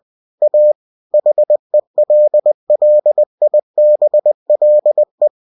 Morse code, heard as a keyed tone, about 20 wpm, 600 Hz. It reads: A HELLIBLE